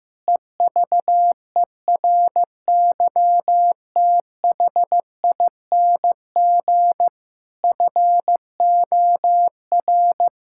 Morse code, heard as a keyed tone, about 15 wpm, 700 Hz. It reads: EVERYTHING FOR
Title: Sentences from Top 1000 Words - Review Part 6 - 15wpm